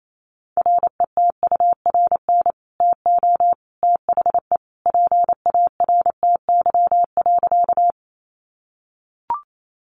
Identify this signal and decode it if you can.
Morse code, heard as a keyed tone, about 28 wpm, 700 Hz. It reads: RETURN TO THE PARTY.
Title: Sentences from Top 1400 Words 28wpm